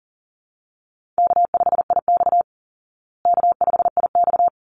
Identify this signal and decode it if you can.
Morse code, heard as a keyed tone, about 40 wpm, 700 Hz. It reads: K5IX K5IX
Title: Call Signs Repeated 3x - 40wpm